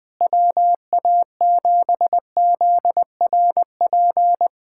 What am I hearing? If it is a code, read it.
Morse code, heard as a keyed tone, about 20 wpm, 700 Hz. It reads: WA7ZRP
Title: USA Callsign DE Callsign - 6 Characters - 20wpm